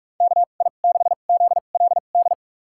Morse code, heard as a keyed tone, about 44 wpm, spaced as for 32 wpm, 700 Hz. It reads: KI6ZLD
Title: USA Call Signs - 6 Characters - 2x Character Spacing - 45wpm